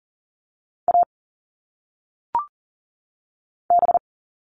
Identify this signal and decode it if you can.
Morse code, heard as a keyed tone, about 40 wpm, 700 Hz. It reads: A B
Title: All Cut Numbers - 1-Digit 40wpm